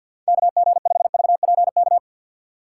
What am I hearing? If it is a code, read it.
Morse code, heard as a keyed tone, about 50 wpm, 700 Hz. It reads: KG5VPK
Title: USA Call Signs - 4x Character Spacing - 50wpm